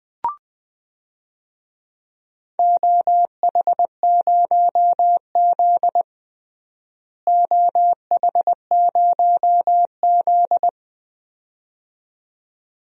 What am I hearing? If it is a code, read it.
Morse code, heard as a keyed tone, about 21 wpm, 700 Hz. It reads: OH0Z OH0Z
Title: DXCC-100 Call Signs - 4 Characters - Repeated 2x - 20wpm